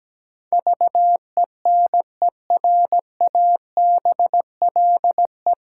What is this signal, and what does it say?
Morse code, heard as a keyed tone, about 17 wpm, 700 Hz. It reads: VENERABLE
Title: Pangram Challenge v2 - 17wpm